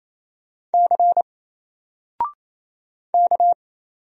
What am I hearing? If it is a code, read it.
Morse code, heard as a keyed tone, about 27 wpm, 700 Hz. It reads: C K
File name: Single-Letter-Number-28wpm